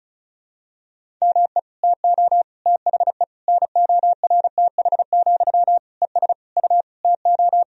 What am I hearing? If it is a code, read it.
Morse code, heard as a keyed tone, about 36 wpm, 700 Hz. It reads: ME TO THE NORTH, ES U TO
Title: Sentences from Top 400 Words and Abbreviations 35wpm